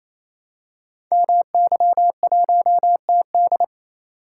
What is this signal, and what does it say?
Morse code, heard as a keyed tone, about 28 wpm, 700 Hz. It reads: MY1TD
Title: Code Groups - 5 Letter:Numbers - 28wpm